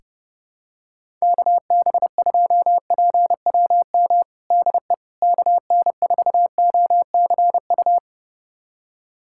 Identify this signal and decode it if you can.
Morse code, heard as a keyed tone, about 32 wpm, 700 Hz. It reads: KB2PWM DE KN4OCU
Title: USA Callsign DE Callsign - 6 Characters - 30wpm